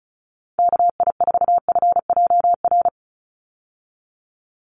Morse code, heard as a keyed tone, about 34 wpm, 700 Hz. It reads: KI4FJR
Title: USA Call Signs 35wpm